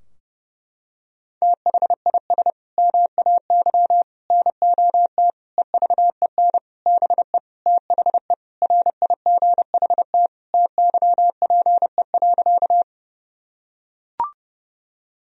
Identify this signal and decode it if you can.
Morse code, heard as a keyed tone, about 30 wpm, 700 Hz. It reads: THIS MAY NOT EVEN BE THE RIGHT TYPE.
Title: Sentences from Top 1000 Words - Review Part 3 - 30wpm